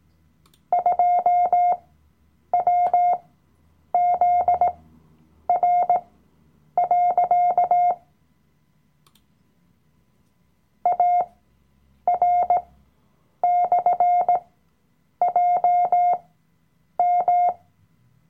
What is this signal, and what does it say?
Morse code, heard as a keyed tone, about 18 wpm, spaced as for 8 wpm, 710 Hz. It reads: 2WZR. AR/JM
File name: Workshop Dateien / CW-Online-Kurs / Lektion 007